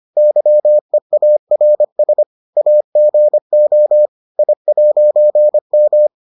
Morse code, heard as a keyed tone, about 25 wpm, 600 Hz. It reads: YEARS AGO I'M